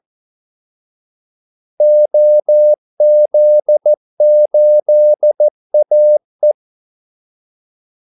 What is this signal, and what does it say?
Morse code, heard as a keyed tone, about 14 wpm, 600 Hz. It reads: OZ8AE
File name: Callsigns-14wpm